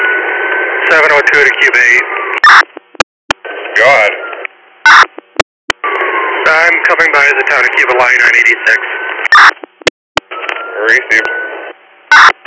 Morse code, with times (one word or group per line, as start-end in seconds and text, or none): none